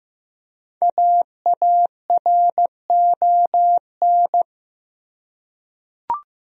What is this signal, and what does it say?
Morse code, heard as a keyed tone, about 15 wpm, 700 Hz. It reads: AARON
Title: USA Common Names - 15wpm